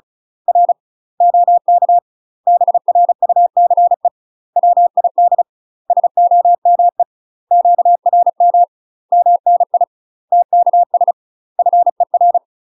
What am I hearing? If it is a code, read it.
Morse code, heard as a keyed tone, about 36 wpm, 700 Hz. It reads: R OK BRUCE WID SOME QRM MNI TKS FER
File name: UK MASTER QSOs TRAINING / 35WPM / UK MASTER QSOs 0031